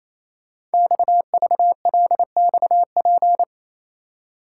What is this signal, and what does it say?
Morse code, heard as a keyed tone, about 29 wpm, 700 Hz. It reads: XVLXP
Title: Code Groups - 5 Letters - 28wpm